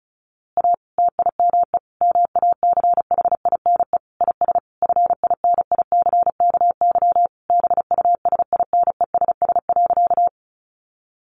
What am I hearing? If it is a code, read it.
Morse code, heard as a keyed tone, about 34 wpm, 700 Hz. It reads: A TIME MACHINE IS FINICKY BUSINESS.
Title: A Sound Of Thunder 35wpm